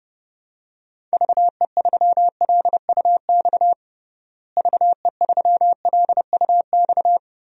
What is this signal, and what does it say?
Morse code, heard as a keyed tone, about 30 wpm, 700 Hz. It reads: VE3LUX VE3LUX
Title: DXCC-100 Call Signs - 6 Characters - Repeated 3x - 30wpm